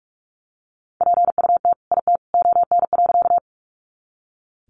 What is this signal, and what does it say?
Morse code, heard as a keyed tone, about 44 wpm, 700 Hz. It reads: PUT IT ON.
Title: Sentences from Top 200 Words 45wpm